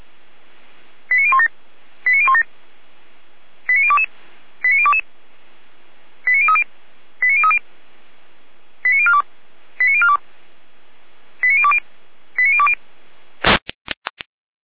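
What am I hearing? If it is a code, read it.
Five-tone selective call (ZVEI), heard as quick runs of five tones, tones of about 70 ms. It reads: 89017 89017 89022 89022 89030 89030 89042 89042 89020 89020
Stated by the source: ZVEI five-tone selective call received on SDR.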